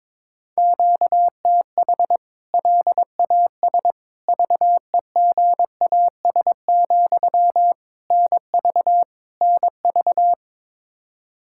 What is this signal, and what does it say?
Morse code, heard as a keyed tone, about 22 wpm, 700 Hz. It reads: QTH LAS VEGAS, NV NV
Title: K1USN SST QSO Practice 22wpm